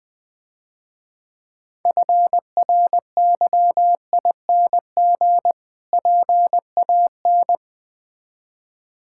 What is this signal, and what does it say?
Morse code, heard as a keyed tone, about 20 wpm, 700 Hz. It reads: FRYING PAN
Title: Difficult Words - Speed-Racing with Hints - 20wpm